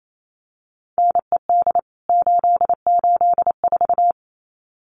Morse code, heard as a keyed tone, about 28 wpm, 700 Hz. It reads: NED 884
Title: CWT Exchange 28wpm